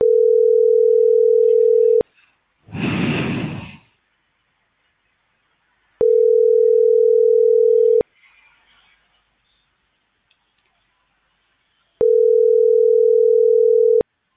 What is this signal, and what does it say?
Ringback tone (North America), 440 + 480 Hz.